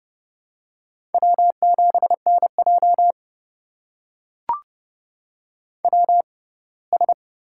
Morse code, heard as a keyed tone, about 30 wpm, spaced as for 20 wpm, 690 Hz. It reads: W7NJ WS
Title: USA Call Signs - 4 Characters - 6x Character Spacing - 30wpm